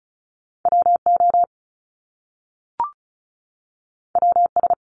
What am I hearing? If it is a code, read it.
Morse code, heard as a keyed tone, about 36 wpm, 700 Hz. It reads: WO WS
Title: Top Two Letter Combinations in Words 35wpm